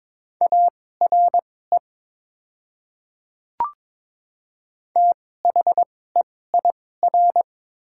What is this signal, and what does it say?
Morse code, heard as a keyed tone, about 22 wpm, spaced as for 16 wpm, 700 Hz. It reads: ARE THEIR
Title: 2x Character Spacing - Top 100 Words - 22wpm